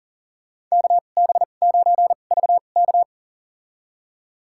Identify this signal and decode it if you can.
Morse code, heard as a keyed tone, about 40 wpm, spaced as for 29 wpm, 700 Hz. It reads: KB9VK
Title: USA Call Signs - 5 Characters - 2x Character Spacing - 40wpm